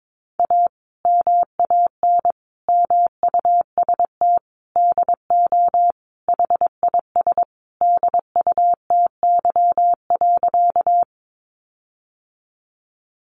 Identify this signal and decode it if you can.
Morse code, heard as a keyed tone, about 22 wpm, 700 Hz. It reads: A MAN MUST DO HIS DUTY.
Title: Sentences from Top 1500 Words 22wpm